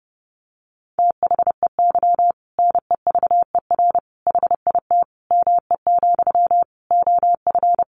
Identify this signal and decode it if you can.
Morse code, heard as a keyed tone, about 30 wpm, 700 Hz. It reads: THEY NEVER HIT ME, OF
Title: Sentences from Top 1200 Words 30wpm